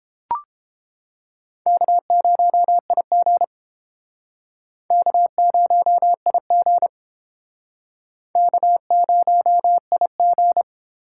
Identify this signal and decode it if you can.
Morse code, heard as a keyed tone, about 30 wpm, 700 Hz. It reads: K0IG K0IG K0IG